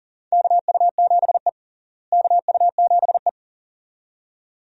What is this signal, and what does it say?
Morse code, heard as a keyed tone, about 40 wpm, 700 Hz. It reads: KU7E KU7E
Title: USA Call Signs - 4 Characters - Repeated 2x - 40wpm